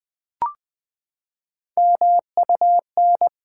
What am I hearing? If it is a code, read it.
Morse code, heard as a keyed tone, about 20 wpm, 700 Hz. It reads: MUN
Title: Build A Word - 3 Letters At A Time - 9-Letter Words - 20wpm